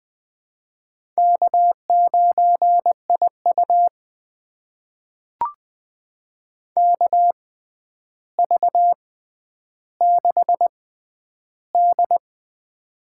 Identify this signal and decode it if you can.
Morse code, heard as a keyed tone, about 20 wpm, spaced as for 10 wpm, 700 Hz. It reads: K9IU KV6D
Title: USA Call Signs - 4 Characters - 6x Character Spacing - 20wpm